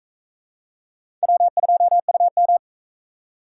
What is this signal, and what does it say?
Morse code, heard as a keyed tone, about 40 wpm, 700 Hz. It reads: W2UM